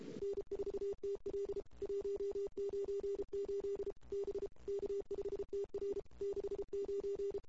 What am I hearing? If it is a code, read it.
Morse code, heard as a keyed tone, about 32 wpm, 400 Hz. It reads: N4TL 198 B K5TR 69
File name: CW / 2008-ss-cw-k5tr-op-n5rz -22:00